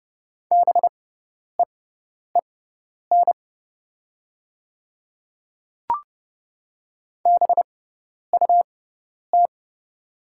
Morse code, heard as a keyed tone, about 30 wpm, spaced as for 10 wpm, 700 Hz. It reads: BEEN BUT
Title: 6x Character Spacing - Top 100 Words - 30wpm